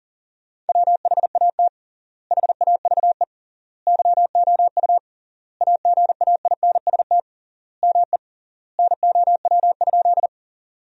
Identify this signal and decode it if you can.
Morse code, heard as a keyed tone, about 37 wpm, 700 Hz. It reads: WHAT HAVE YOU AGAINST ME NOW?
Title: 3x Word-Spacing - Sentences from Top 400 Words - 40wpm